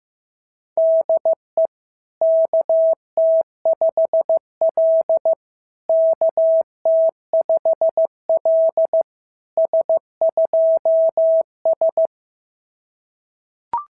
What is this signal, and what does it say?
Morse code, heard as a keyed tone, about 14 wpm, 650 Hz. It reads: DE KT5L KT5L S2S
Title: SOTA QSO Practice 15wpm